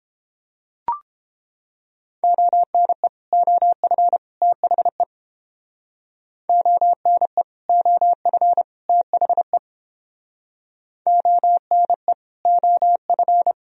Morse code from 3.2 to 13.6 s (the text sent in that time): OF THE ONE OF THE ONE OF